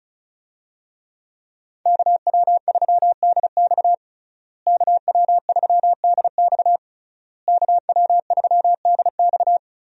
Morse code, heard as a keyed tone, about 35 wpm, 700 Hz. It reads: KW3DX KW3DX KW3DX
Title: USA Call Signs - 5 Characters - Repeated 3x - 35wpm